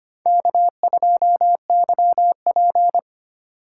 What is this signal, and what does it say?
Morse code, heard as a keyed tone, about 25 wpm, 700 Hz. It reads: K2YP
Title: USA Call Signs - 3x Character Spacing - 25wpm